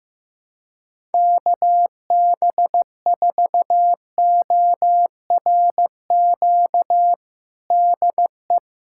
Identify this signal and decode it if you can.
Morse code, heard as a keyed tone, about 15 wpm, 700 Hz. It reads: KB4ORQ DE
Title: USA Callsign DE Callsign - 6 Characters - 15wpm